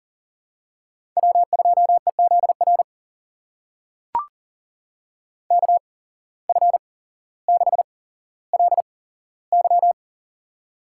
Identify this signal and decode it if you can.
Morse code, heard as a keyed tone, about 40 wpm, spaced as for 17 wpm, 700 Hz. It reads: W2EZR KF6LY
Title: USA Call Signs - 5 Characters - 8x Character Spacing - 40wpm